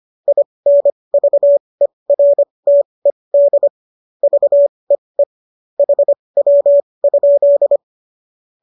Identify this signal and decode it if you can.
Morse code, heard as a keyed tone, about 25 wpm, spaced as for 20 wpm, 570 Hz. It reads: INVERTED VEE HW?
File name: qso223 20